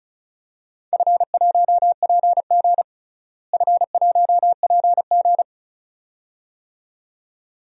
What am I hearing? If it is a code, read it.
Morse code, heard as a keyed tone, about 34 wpm, 700 Hz. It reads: F1PG F1PG
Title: DXCC-100 Call Signs - 4 Characters - Repeated 2x - 35wpm